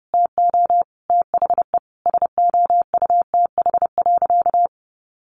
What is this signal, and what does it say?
Morse code, heard as a keyed tone, about 30 wpm, 700 Hz. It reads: TO THE SOUTH.
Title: Sentences from Top 400 Words 30wpm